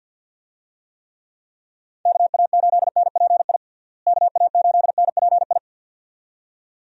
Morse code, heard as a keyed tone, about 49 wpm, 700 Hz. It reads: KA8NPI KA8NPI
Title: USA Call Signs - 6 Characters - Repeated 2x - 50wpm